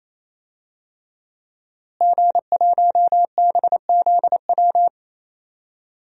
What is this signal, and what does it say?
Morse code, heard as a keyed tone, about 28 wpm, 700 Hz. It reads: G1BZW
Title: DXCC-100 Call Signs - 5 Characters - Repeated 3x - 28wpm